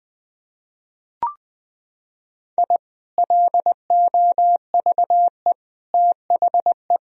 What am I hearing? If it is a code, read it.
Morse code, heard as a keyed tone, about 20 wpm, 700 Hz. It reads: I LOVE THE
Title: Sentences from Top 1000 Words - Review Part 3 - 20wpm